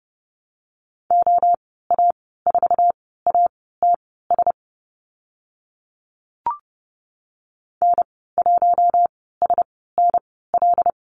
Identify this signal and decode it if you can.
Morse code, heard as a keyed tone, about 30 wpm, spaced as for 17 wpm, 700 Hz. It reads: OA4ATS N1SNL
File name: Lesson 14g - Number 2 - Call Signs 30wpm